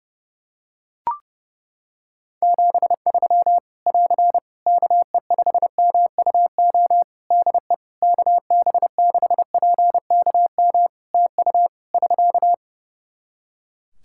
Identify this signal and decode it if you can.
Morse code, heard as a keyed tone, about 30 wpm, 700 Hz. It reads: 73 <AR> KE5MUO DE KB6PKM TU <SK>
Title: Practice QSO - Exchange 4 - Ending - 30wpm